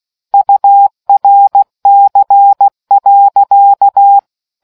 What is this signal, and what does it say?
Morse code, heard as a keyed tone, about 16 wpm, 800 Hz. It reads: URC.